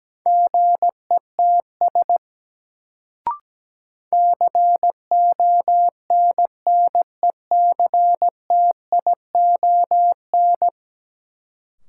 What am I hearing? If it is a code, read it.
Morse code, heard as a keyed tone, about 17 wpm, 700 Hz. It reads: GETS CONNECTION
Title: Sentences from Top 1300 Words - New Word Drill - 17wpm